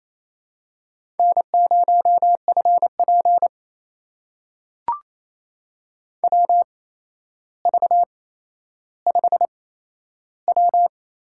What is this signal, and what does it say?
Morse code, heard as a keyed tone, about 27 wpm, spaced as for 12 wpm, 700 Hz. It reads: N0FP WV5W
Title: USA Call Signs - 4 Characters - 8x Character Spacing - 28wpm